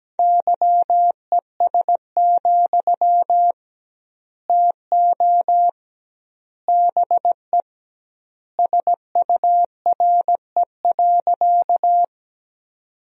Morse code, heard as a keyed tone, about 17 wpm, 700 Hz. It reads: YES, TO BE SURE.
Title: 2x Word-Spacing - Sentences from Top 500 Words - 17wpm